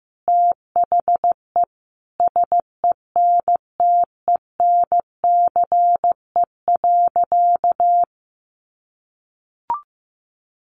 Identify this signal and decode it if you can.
Morse code, heard as a keyed tone, about 15 wpm, 700 Hz. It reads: THE SENTENCE.
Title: Sentences from Top 1200 Words 15wpm